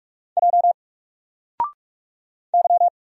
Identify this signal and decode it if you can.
Morse code, heard as a keyed tone, about 44 wpm, 700 Hz. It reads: J Y
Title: Single Letter-Number 45wpm